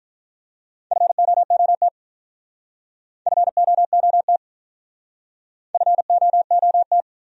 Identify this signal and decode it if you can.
Morse code, heard as a keyed tone, about 47 wpm, 700 Hz. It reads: FOOT FOOT FOOT